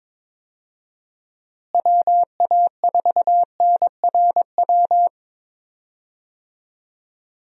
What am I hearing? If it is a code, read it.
Morse code, heard as a keyed tone, about 23 wpm, 700 Hz. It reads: WA4NRW
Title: USA Call Signs - 6 Characters - Repeated 3x - 22wpm